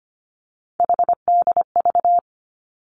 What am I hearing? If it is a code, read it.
Morse code, heard as a keyed tone, about 25 wpm, 700 Hz. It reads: HDV
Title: Three Challenging Letter-Number Combinations 25wpm